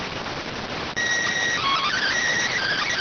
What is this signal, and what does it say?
SSTV (slow-scan television), PD 290.